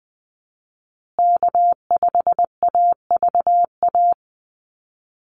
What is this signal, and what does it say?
Morse code, heard as a keyed tone, about 20 wpm, 700 Hz. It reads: K5AVA
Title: USA Call Signs Repeated 3x 20wpm